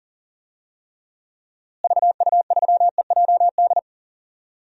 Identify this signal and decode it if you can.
Morse code, heard as a keyed tone, about 40 wpm, 700 Hz. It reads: VU3EJD
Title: DXCC-100 Call Signs - 6 Characters - Repeated 3x - 40wpm